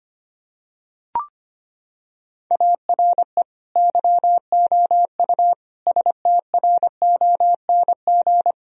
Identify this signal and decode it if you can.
Morse code, heard as a keyed tone, about 25 wpm, 700 Hz. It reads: ARE YOU STRONG
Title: Sentences from Top 1000 Words - Review Part 3 - 25wpm